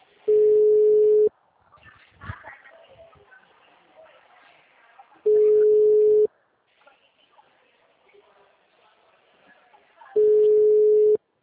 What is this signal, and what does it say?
Ringback tone (Europe, 425 Hz), with speech.